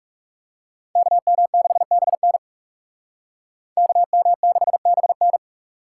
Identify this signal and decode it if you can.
Morse code, heard as a keyed tone, about 38 wpm, 700 Hz. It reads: KM6BN KM6BN